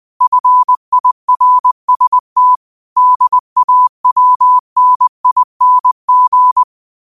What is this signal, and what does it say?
Morse code, heard as a keyed tone, about 20 wpm, 1000 Hz. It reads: FIRST DAWNING
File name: qotdfiles / 20wpm